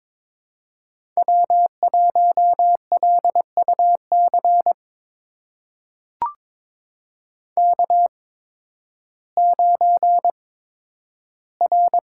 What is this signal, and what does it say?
Morse code, heard as a keyed tone, about 22 wpm, spaced as for 12 wpm, 700 Hz. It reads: W1LUC K9R
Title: USA Call Signs - 5 Characters - 8x Character Spacing - 22wpm